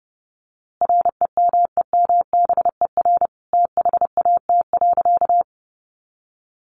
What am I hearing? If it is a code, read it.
Morse code, heard as a keyed tone, about 30 wpm, 700 Hz. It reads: REMEMBER THAT.
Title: A Sound Of Thunder 30wpm